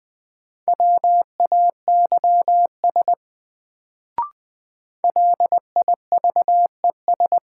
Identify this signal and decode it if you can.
Morse code, heard as a keyed tone, about 20 wpm, 700 Hz. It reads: WAYS LIVES
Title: Sets of 5 Words Encore - New Word Drill - 20wpm